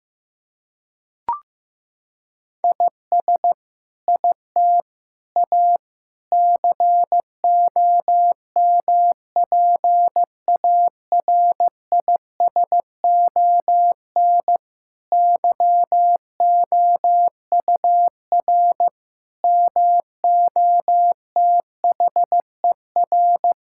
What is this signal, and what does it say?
Morse code, heard as a keyed tone, about 15 wpm, 700 Hz. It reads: IS IT A COMPARISON YOUR MOTHER
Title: Sentences from Top 1600 Words 15wpm